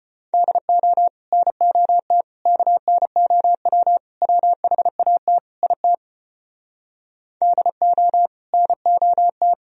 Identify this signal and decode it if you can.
Morse code, heard as a keyed tone, about 33 wpm, 700 Hz. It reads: DO NOT KNOW WHAT IT DO NOT